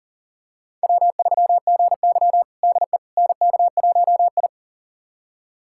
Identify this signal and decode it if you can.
Morse code, heard as a keyed tone, about 40 wpm, 700 Hz. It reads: W3GY DE NK1I
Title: USA Callsign DE Callsign - 4 Characters - 40wpm